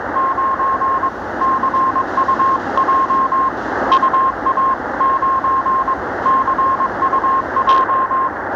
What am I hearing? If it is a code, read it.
Morse code, heard as a keyed tone, about 21 wpm, 1060 Hz, with radio noise. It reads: 9CUJ UA9CUJ
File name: attachments / recording